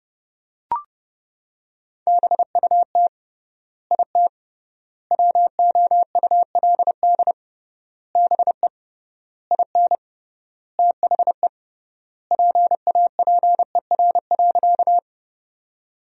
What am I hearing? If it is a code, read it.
Morse code, heard as a keyed tone, about 30 wpm, 700 Hz. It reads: BUT IT WOULD BE IN THE PAPER.
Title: 3x Word-Spacing - Sentences from Top 300 Words - 30wpm